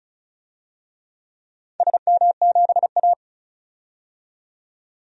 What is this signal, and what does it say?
Morse code, heard as a keyed tone, about 36 wpm, 700 Hz. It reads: SM7A
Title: DXCC-100 Call Signs - 4 Characters - 35wpm